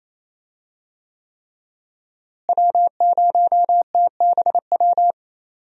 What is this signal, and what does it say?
Morse code, heard as a keyed tone, about 27 wpm, 700 Hz. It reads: W0TBW
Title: USA Call Signs - 5 Characters - Repeated 3x - 28wpm